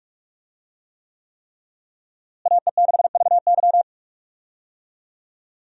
Morse code, heard as a keyed tone, about 46 wpm, 700 Hz. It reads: AE6VY
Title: USA Call Signs - 5 Characters - Repeated 3x - 45wpm